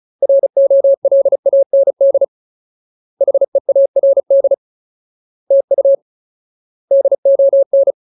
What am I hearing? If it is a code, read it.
Morse code, heard as a keyed tone, about 34 wpm, 550 Hz. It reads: ROLAND HEARD TU DON
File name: Ultimate Word List / 35WPM Ultimate